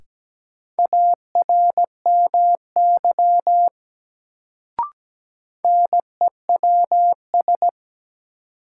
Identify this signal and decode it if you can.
Morse code, heard as a keyed tone, about 17 wpm, 700 Hz. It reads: ARMY NEWS
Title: Sentences from Top 800 Words - New Word Drill - 17wpm